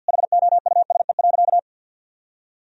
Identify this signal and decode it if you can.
Morse code, heard as a keyed tone, about 52 wpm, 700 Hz. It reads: HOUSE.
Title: Sentences from Top 1000 Words - Review Part 3 - 50wpm